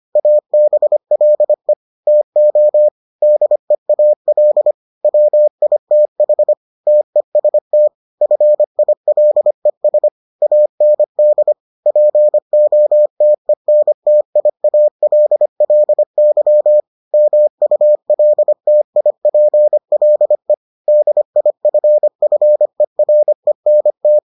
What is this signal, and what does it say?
Morse code, heard as a keyed tone, about 25 wpm, 600 Hz. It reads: ABLE TO DEAL WITH TEST FILES AND POTENTIALLY MULTIPLE DIFFERENT